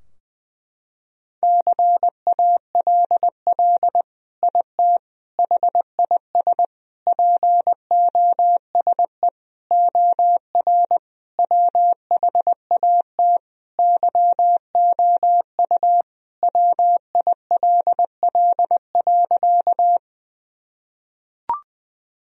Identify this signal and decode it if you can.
Morse code, heard as a keyed tone, about 20 wpm, 700 Hz. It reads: CALL IT HIS POSE OR WHAT YOU WILL.
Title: Sentences from Top 1000 Words - Review Part 2 - 20wpm